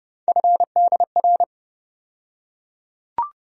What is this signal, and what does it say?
Morse code, heard as a keyed tone, about 32 wpm, 700 Hz. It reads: FDR
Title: USA Call Sign Postfixes - 3 Letters 30wpm